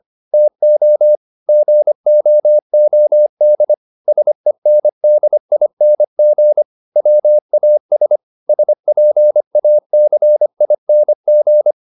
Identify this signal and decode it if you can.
Morse code, heard as a keyed tone, about 25 wpm, 600 Hz. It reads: TO GOOD SENDING WAS SPACING